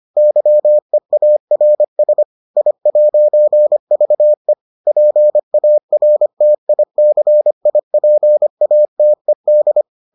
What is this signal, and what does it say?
Morse code, heard as a keyed tone, about 25 wpm, 600 Hz. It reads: YEARS I'VE PARTICIPATED